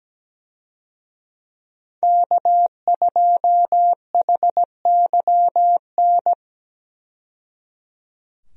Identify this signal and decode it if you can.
Morse code, heard as a keyed tone, about 17 wpm, 700 Hz. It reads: K2HYN
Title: USA Call Signs - 5 Characters - 17wpm